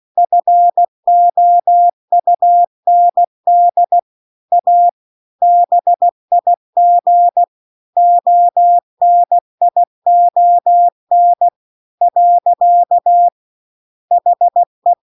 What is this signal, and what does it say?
Morse code, heard as a keyed tone, about 16 wpm, 700 Hz. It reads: FOUND A BIG ONION . HE